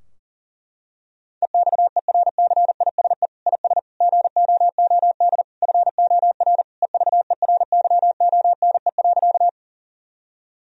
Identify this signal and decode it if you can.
Morse code, heard as a keyed tone, about 40 wpm, 700 Hz. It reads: EXERCISE IS GOOD FOR EVERYONE.
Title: Sentences from Top 1000 Words - Review Part 4 - 40wpm